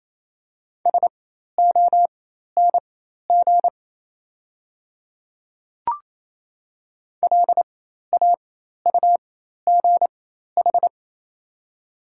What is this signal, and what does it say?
Morse code, heard as a keyed tone, about 28 wpm, spaced as for 13 wpm, 690 Hz. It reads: SONG LAUGH
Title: 4x Character Spacing - Top 500 Words - 28wpm